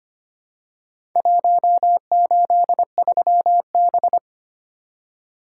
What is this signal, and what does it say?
Morse code, heard as a keyed tone, about 25 wpm, 700 Hz. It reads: 183B
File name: practice / 25wpm / Four Letter-Number Combinations - 25wpm